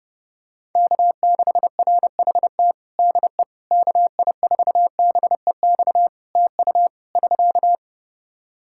Morse code, heard as a keyed tone, about 30 wpm, 700 Hz. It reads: K6RHT DE KI4BEX TU <SK>
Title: K1USN SST QSO Practice 30wpm